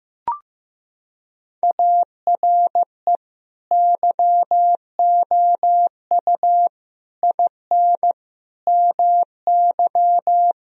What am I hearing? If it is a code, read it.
Morse code, heard as a keyed tone, about 15 wpm, 700 Hz. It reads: ARE YOU IN MY
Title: Sentences from Top 1000 Words - Review Part 2 - 15wpm